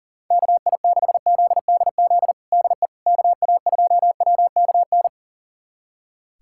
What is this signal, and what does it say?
Morse code, heard as a keyed tone, about 40 wpm, 700 Hz. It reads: KI6ZDZ DE KA2WKN
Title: USA Callsign DE Callsign 40wpm